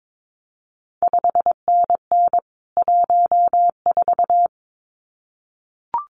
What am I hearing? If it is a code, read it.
Morse code, heard as a keyed tone, about 22 wpm, 700 Hz. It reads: 5NN 14
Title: CQ WW DX Contest Exchange - 22wpm